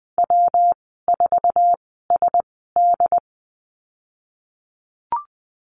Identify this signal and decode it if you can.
Morse code, heard as a keyed tone, about 20 wpm, spaced as for 14 wpm, 700 Hz. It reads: W4SD
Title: 122 Lesson 20g - Number 3 - Call Signs 20wpm